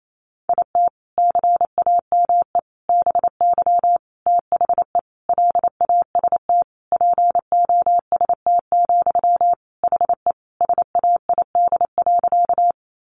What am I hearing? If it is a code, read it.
Morse code, heard as a keyed tone, about 28 wpm, 700 Hz. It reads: IT CAME BY THE LAST POST, HE SAID.
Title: Sentences from Top 1000 Words 28wpm